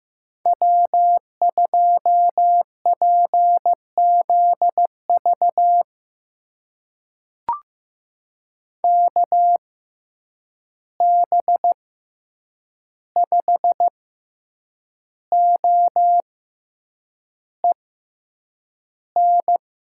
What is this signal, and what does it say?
Morse code, heard as a keyed tone, about 15 wpm, spaced as for 7 wpm, 700 Hz. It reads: W2PZV KB5OEN
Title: USA Call Signs - 6x Character Spacing - 15wpm